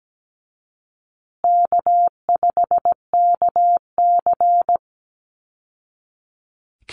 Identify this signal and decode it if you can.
Morse code, heard as a keyed tone, about 17 wpm, 700 Hz. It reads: K5KC
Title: Easy Call Signs Repeated 3x - 17wpm